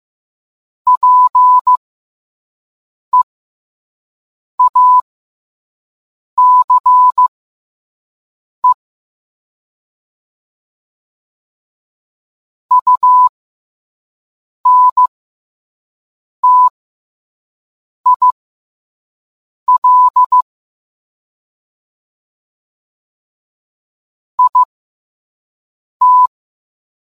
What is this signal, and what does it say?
Morse code, heard as a keyed tone, about 15 wpm, spaced as for 5 wpm, 1000 Hz. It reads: PEACE UNTIL IT